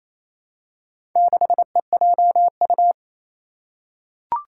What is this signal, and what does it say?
Morse code, heard as a keyed tone, about 28 wpm, 700 Hz. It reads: 6EJU
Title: Four Challenging Letter-Number Combinations 28wpm